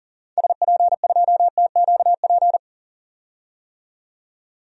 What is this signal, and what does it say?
Morse code, heard as a keyed tone, about 40 wpm, 700 Hz. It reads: SP2TQP
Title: DXCC-100 Call Signs - 6 Characters - 40wpm